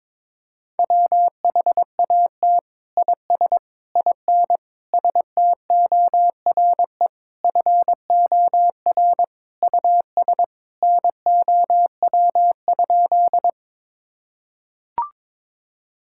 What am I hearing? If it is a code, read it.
Morse code, heard as a keyed tone, about 22 wpm, 700 Hz. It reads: WHAT IS IN STORE FOR US NOW?
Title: Sentences from Top 1000 Words - Review Part 5 - 22wpm